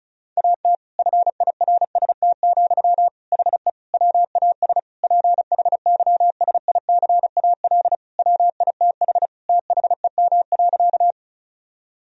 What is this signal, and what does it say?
Morse code, heard as a keyed tone, about 36 wpm, 700 Hz. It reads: AT FIRST, HE WAS PHYSICAL WITH THEM.
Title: Sentences from Top 1000 Words - Review Part 4 - 35wpm